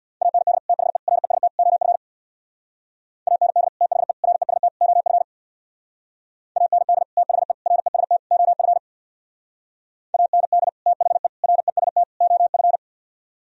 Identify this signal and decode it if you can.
Morse code, heard as a keyed tone, about 61 wpm, 700 Hz. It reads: AND THE REST OF AND THE REST OF AND THE REST OF AND THE REST OF
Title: Speed-Racing - Sets of 4 Words 50wpm